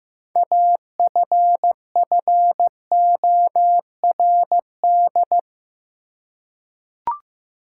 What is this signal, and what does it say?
Morse code, heard as a keyed tone, about 16 wpm, 700 Hz. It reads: AFFORD
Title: Six Letter Words - 15wpm